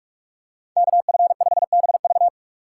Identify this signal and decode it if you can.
Morse code, heard as a keyed tone, about 46 wpm, 700 Hz. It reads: KF5BV